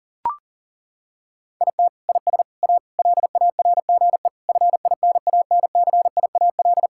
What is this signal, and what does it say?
Morse code, heard as a keyed tone, about 40 wpm, 700 Hz. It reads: IT IS A LARGE FINANCIAL
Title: Sentences from Top 1000 Words - Review Part 5 - 40wpm